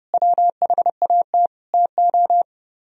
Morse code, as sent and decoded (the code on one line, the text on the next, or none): .-- .... .- - / - ---
WHAT TO